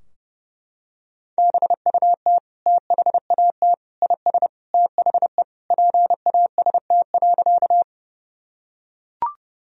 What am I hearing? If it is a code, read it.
Morse code, heard as a keyed tone, about 29 wpm, 700 Hz. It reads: BUT THAT IS THE PAST.
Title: Sentences from Top 1000 Words - Review Part 3 - 30wpm